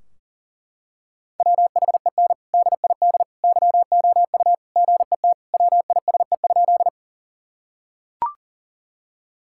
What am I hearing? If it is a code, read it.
Morse code, heard as a keyed tone, about 40 wpm, 700 Hz. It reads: WHEN DID YOU GET WISE?
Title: Sentences from Top 1600 Words 40wpm